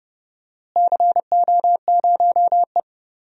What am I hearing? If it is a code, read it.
Morse code, heard as a keyed tone, about 29 wpm, 700 Hz. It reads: CO0E
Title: DXCC-100 Call Signs - 4 Characters - Repeated 3x - 30wpm